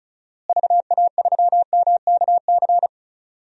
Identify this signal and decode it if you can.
Morse code, heard as a keyed tone, about 36 wpm, 700 Hz. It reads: VA3MKC
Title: DXCC-100 Call Signs - 6 Characters - Repeated 3x - 35wpm